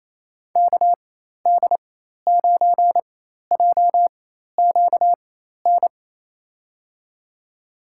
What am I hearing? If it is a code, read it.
Morse code, heard as a keyed tone, about 28 wpm, spaced as for 13 wpm, 700 Hz. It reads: KD9JQN
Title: USA Call Signs - 6 Characters - 4x Character Spacing - 28wpm